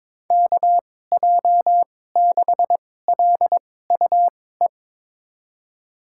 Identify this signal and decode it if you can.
Morse code, heard as a keyed tone, about 22 wpm, spaced as for 16 wpm, 700 Hz. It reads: KJ6LUE
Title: USA Call Signs - 6 Characters - 2x Character Spacing - 22wpm